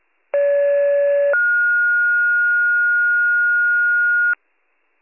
Two-tone page, heard as a one-second tone, then a three-second tone, about 570 then 1430 Hz. It is heard from 0.3 to 4.3 s.